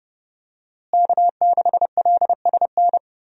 Morse code, heard as a keyed tone, about 30 wpm, 700 Hz. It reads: K6LSN
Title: USA Call Signs - 5 Characters - Repeated 2x - 30wpm